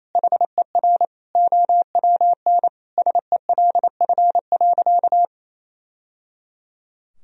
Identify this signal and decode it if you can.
Morse code, heard as a keyed tone, about 28 wpm, 700 Hz. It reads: HER OWN SELF.
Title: Sentences from Top 1000 Words - Review Part 1 - 28wpm